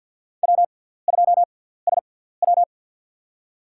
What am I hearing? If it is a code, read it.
Morse code, heard as a keyed tone, about 53 wpm, spaced as for 17 wpm, 700 Hz. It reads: W2SW
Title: USA Call Signs - 6x Character Spacing - 50wpm